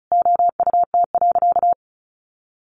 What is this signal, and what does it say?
Morse code, heard as a keyed tone, about 34 wpm, 700 Hz. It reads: OUT.
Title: A Sound Of Thunder 35wpm